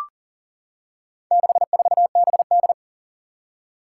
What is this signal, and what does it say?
Morse code, heard as a keyed tone, about 40 wpm, 700 Hz. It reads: 64BD
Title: Four Challenging Letter-Number Combinations 40wpm